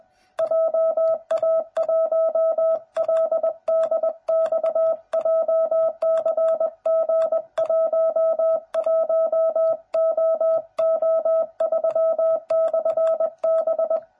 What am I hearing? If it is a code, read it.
Morse code, heard as a keyed tone, about 20 wpm, 660 Hz. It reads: JA1LDX JCG11OO3/B
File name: jh8cbh / 2021cw / cw09b